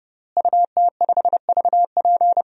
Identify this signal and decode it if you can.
Morse code, heard as a keyed tone, about 30 wpm, 700 Hz. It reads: UT5VP
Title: DXCC-100 Call Signs - 5 Characters - Repeated 3x - 30wpm